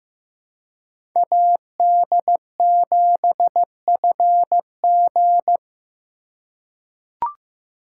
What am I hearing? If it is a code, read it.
Morse code, heard as a keyed tone, about 15 wpm, 700 Hz. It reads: AD7FG
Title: USA Call Signs - 3x Character Spacing - 15wpm